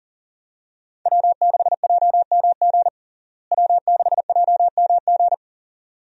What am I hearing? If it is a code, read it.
Morse code, heard as a keyed tone, about 37 wpm, 700 Hz. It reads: W6JMG W6JMG
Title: USA Call Signs - 5 Characters - Repeated 2x - 40wpm